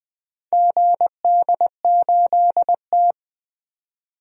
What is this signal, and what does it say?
Morse code, heard as a keyed tone, about 20 wpm, 700 Hz. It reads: GD8T
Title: DXCC-100 Call Signs - 4 Characters - 20wpm